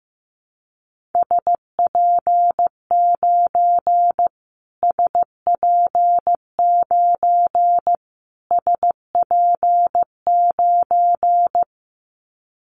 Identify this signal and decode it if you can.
Morse code, heard as a keyed tone, about 15 wpm, 700 Hz. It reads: SP9 SP9 SP9
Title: Callsign Prefixes for Top 25 DXCC - Repeated 3x - 15wpm